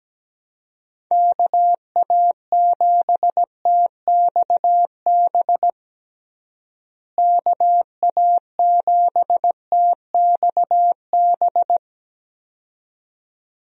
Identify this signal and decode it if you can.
Morse code, heard as a keyed tone, about 17 wpm, 700 Hz. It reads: KA7TXB KA7TXB
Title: USA Call Signs - 6 Characters - Repeated 2x - 17wpm